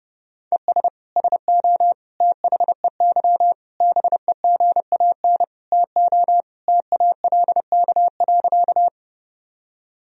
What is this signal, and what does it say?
Morse code, heard as a keyed tone, about 30 wpm, 700 Hz. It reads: ES SO THEY BEGAN TO TALK.
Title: Sentences from Top 400 Words and Abbreviations 30wpm